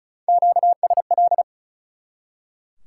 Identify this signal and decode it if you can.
Morse code, heard as a keyed tone, about 47 wpm, spaced as for 41 wpm, 700 Hz. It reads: QSL
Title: Speed-Racing Top 90 QSO Elements 35wpm